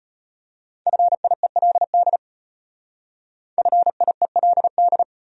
Morse code, heard as a keyed tone, about 31 wpm, 700 Hz. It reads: FIELD FIELD
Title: Speed-Racing Top 500 Words 25wpm